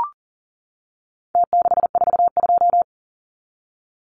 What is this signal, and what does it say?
Morse code, heard as a keyed tone, about 40 wpm, 700 Hz. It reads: T642
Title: Numbers - Contest Style - 4-Digit 40wpm